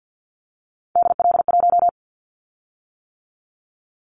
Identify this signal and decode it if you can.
Morse code, heard as a keyed tone, about 50 wpm, 700 Hz. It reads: DL1
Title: Callsign Prefixes for Top 50 DXCC - 50wpm